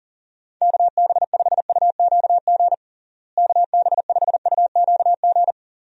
Morse code, heard as a keyed tone, about 40 wpm, 700 Hz. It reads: KB5UQG KB5UQG
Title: USA Call Signs - 6 Characters - Repeated 3x - 40wpm